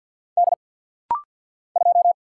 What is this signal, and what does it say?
Morse code, heard as a keyed tone, about 50 wpm, 700 Hz. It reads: D 2
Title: Single Letter-Number - Rapid-Fire 50wpm